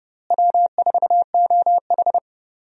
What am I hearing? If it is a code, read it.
Morse code, heard as a keyed tone, about 30 wpm, 700 Hz. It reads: W4OH
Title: USA Call Signs - 4 Characters - Repeated 3x - 30wpm